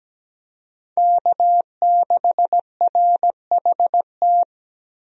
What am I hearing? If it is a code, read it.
Morse code, heard as a keyed tone, about 17 wpm, 700 Hz. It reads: K6RHT
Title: USA Call Signs - 5 Characters - 17wpm